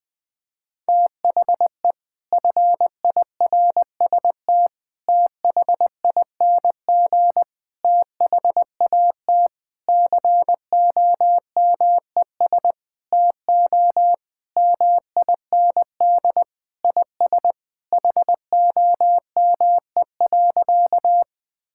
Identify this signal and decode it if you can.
Morse code, heard as a keyed tone, about 20 wpm, 700 Hz. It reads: THE FIRST THING THAT COMES TO MIND IS HOME.
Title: Sentences with 7-Word Phrases - Spoken Beginning - 20wpm